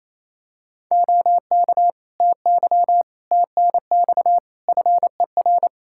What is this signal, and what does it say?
Morse code, heard as a keyed tone, about 28 wpm, 700 Hz. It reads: OK TY TNX FER
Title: K1USN SST QSO Practice 28wpm